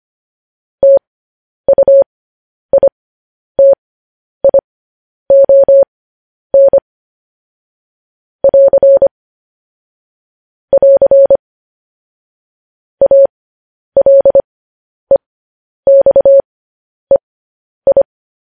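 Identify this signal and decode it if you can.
Morse code, heard as a keyed tone, about 25 wpm, spaced as for 10 wpm, 550 Hz. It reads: TUITION <AR> <AR> ALEXEI
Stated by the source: Morse code news headlines at 10.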